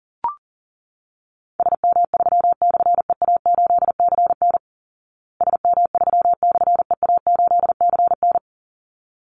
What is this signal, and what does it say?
Morse code, heard as a keyed tone, about 40 wpm, 700 Hz. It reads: SM3/EA8CN SM3/EA8CN
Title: Call Signs Repeated 3x - 40wpm